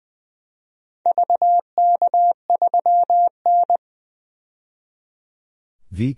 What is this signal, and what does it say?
Morse code, heard as a keyed tone, about 20 wpm, 700 Hz. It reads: VK3N
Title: DXCC-100 Call Signs - 4 Characters - 20wpm